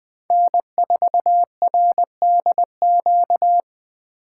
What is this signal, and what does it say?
Morse code, heard as a keyed tone, about 20 wpm, 700 Hz. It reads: N4RDQ